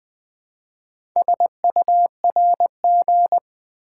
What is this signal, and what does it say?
Morse code, heard as a keyed tone, about 20 wpm, 700 Hz. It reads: SURG
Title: Build A Word - 4 Letters At A Time - 8-Letter Words - 20wpm